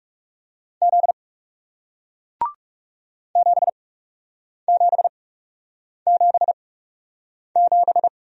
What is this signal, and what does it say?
Morse code, heard as a keyed tone, about 40 wpm, spaced as for 8 wpm, 700 Hz. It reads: Z 7777